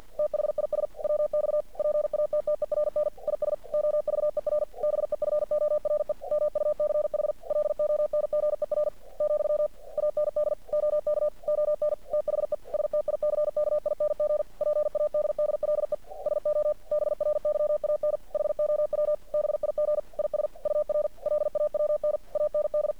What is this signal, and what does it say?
Morse code, heard as a keyed tone, about 50 wpm, 600 Hz. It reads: THIS WX PATTERN IS OVER BEFORE MUCH LONGER , AND OK ON THE STICKING PADDLE SO BRYAN HOW BIG IS UR LAWN AND
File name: archived / CRSnet